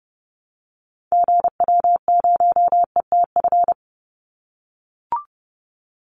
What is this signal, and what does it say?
Morse code, heard as a keyed tone, about 30 wpm, 700 Hz. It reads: GW0ETF
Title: Call Signs - 30wpm